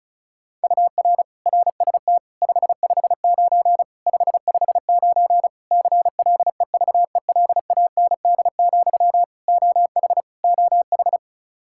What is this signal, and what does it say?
Morse code, heard as a keyed tone, about 36 wpm, 700 Hz. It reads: UR RST 559 559 CLEVELAND, OH OH
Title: K1USN SST QSO Practice 35wpm